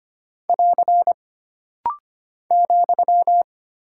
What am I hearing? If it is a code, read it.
Morse code, heard as a keyed tone, about 25 wpm, 700 Hz. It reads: <AR> ,
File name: Single Punctuation-Prosign - Rapid-Fire - 25wpm